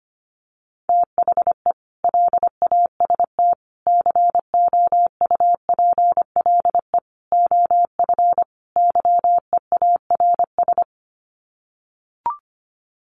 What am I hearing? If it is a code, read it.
Morse code, heard as a keyed tone, about 25 wpm, 700 Hz. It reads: THE LAST COUPLE OF YEARS
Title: Sets of 5 Words 25wpm